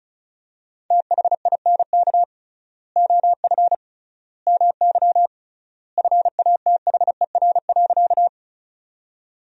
Morse code, heard as a keyed tone, about 36 wpm, 700 Hz. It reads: THINK OF MY FATHER.
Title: 3x Word-Spacing - Sentences from Top 300 Words - 35wpm